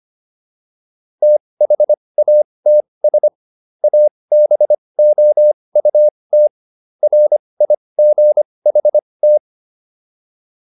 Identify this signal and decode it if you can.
Morse code, heard as a keyed tone, about 25 wpm, spaced as for 20 wpm, 600 Hz. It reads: THATS ABOUT RIGHT